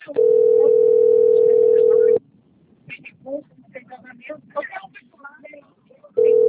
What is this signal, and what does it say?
Ringback tone (North America), 440 + 480 Hz, with speech.
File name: rg-1600-unknown- - - .31746